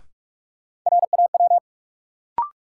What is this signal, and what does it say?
Morse code, heard as a keyed tone, about 46 wpm, 700 Hz. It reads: RAW
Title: Sentences from Top 1500 Words - New Word Drill - 45wpm